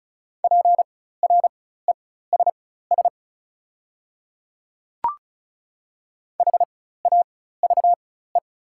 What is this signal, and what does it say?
Morse code, heard as a keyed tone, about 34 wpm, spaced as for 16 wpm, 690 Hz. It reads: PRESS HAVE
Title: 4x Character Spacing - Top 500 Words - 35wpm